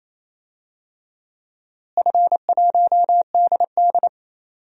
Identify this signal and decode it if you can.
Morse code, heard as a keyed tone, about 28 wpm, 700 Hz. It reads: F1DD
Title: DXCC-100 Call Signs - 4 Characters - Repeated 2x - 28wpm